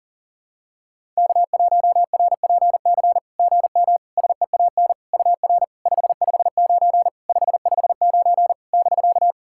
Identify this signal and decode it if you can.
Morse code, heard as a keyed tone, about 44 wpm, 700 Hz. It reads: K1RPC GM SEAN UR 559 559 <BK>
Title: SOTA QSO Practice 40wpm